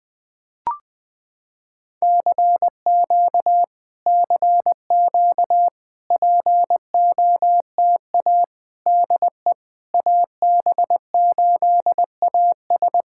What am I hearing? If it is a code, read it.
Morse code, heard as a keyed tone, about 21 wpm, 700 Hz. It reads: CQ CQ POTA DE AB8AS
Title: POTA QSO Practice - 20wpm